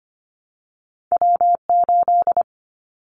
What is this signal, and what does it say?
Morse code, heard as a keyed tone, about 25 wpm, 700 Hz. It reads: W8
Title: Callsign Prefixes for Top 25 DXCC - Repeated 3x - 25wpm